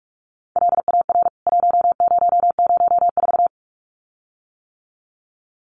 Morse code, heard as a keyed tone, about 44 wpm, 700 Hz. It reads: LAR 1004
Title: CWT Exchange 45wpm